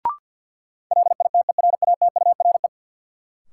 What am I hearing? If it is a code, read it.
Morse code, heard as a keyed tone, about 50 wpm, 700 Hz. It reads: LITERATURE
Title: Sentences from Top 600 Words - New Word Drill - 50wpm